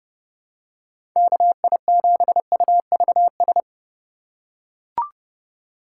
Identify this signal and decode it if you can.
Morse code, heard as a keyed tone, about 30 wpm, 700 Hz. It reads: KI7UVS